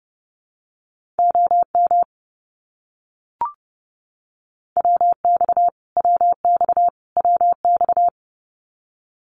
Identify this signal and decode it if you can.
Morse code, heard as a keyed tone, about 30 wpm, 700 Hz. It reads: OM WX WX WX
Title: Abbreviations Repeated 3x - 30wpm